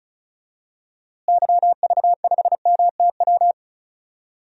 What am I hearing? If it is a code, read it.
Morse code, heard as a keyed tone, about 34 wpm, 700 Hz. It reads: YV5MTW
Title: DXCC-100 Call Signs - 6 Characters - Repeated 2x - 35wpm